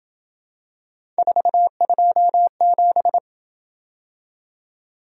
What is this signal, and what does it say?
Morse code, heard as a keyed tone, about 27 wpm, 700 Hz. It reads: 427